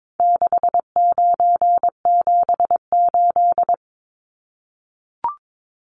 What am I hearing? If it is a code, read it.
Morse code, heard as a keyed tone, about 22 wpm, 700 Hz. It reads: 6978
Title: Numbers - 4-Digit 22wpm